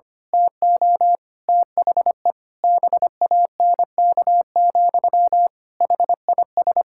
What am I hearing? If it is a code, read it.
Morse code, heard as a keyed tone, about 25 wpm, 700 Hz. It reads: TO THE BANK, HIS
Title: Pangram Challenge v2 - 25wpm